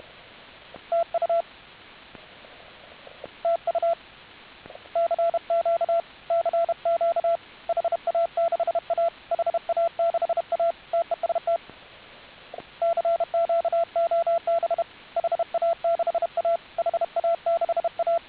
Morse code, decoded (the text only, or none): TU TU CQ CQ HA6A HA6A TEST CQ OB HA6A HA6A